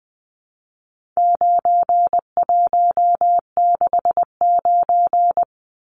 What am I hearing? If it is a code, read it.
Morse code, heard as a keyed tone, about 20 wpm, 700 Hz. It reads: 9169
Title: Numbers - 4-Digit 20wpm